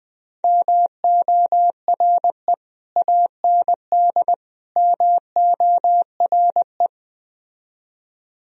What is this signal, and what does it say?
Morse code, heard as a keyed tone, about 21 wpm, 700 Hz. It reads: MORE AND MORE
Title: Speed-Racing - Sets of 3 Words 17wpm